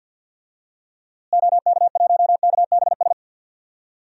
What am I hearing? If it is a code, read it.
Morse code, heard as a keyed tone, about 53 wpm, 700 Hz. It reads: OK1KBS
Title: DXCC-100 Call Signs - 6 Characters - Repeated 2x - 50wpm